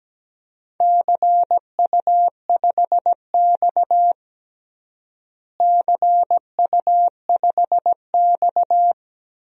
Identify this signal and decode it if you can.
Morse code, heard as a keyed tone, about 17 wpm, 700 Hz. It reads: CU5X CU5X
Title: DXCC-100 Call Signs - 4 Characters - Repeated 2x - 17wpm